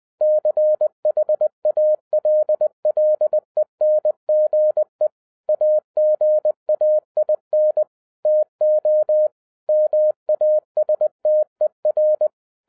Morse code, heard as a keyed tone, about 20 wpm, 610 Hz. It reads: CHALLENGE AGAIN TO MASTER
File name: 2012-01-mp3 / n05-CAME BACK TO HAM RADIO AFTER 30 YEARS OF INTERVAL-20wpm